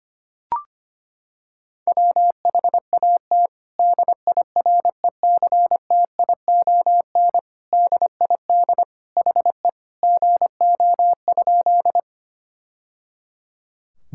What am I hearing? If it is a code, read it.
Morse code, heard as a keyed tone, about 25 wpm, 700 Hz. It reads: WHAT DIRECTION DID HE GO?
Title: Sentences from Top 1000 Words - Review Part 5 - 25wpm